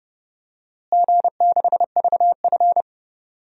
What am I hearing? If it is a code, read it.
Morse code, heard as a keyed tone, about 30 wpm, 700 Hz. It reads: G6VF
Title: DXCC-100 Call Signs - 4 Characters - Repeated 3x - 30wpm